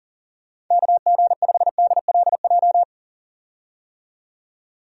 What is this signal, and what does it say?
Morse code, heard as a keyed tone, about 40 wpm, 700 Hz. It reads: KG5DLJ
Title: USA Call Signs - 6 Characters - 40wpm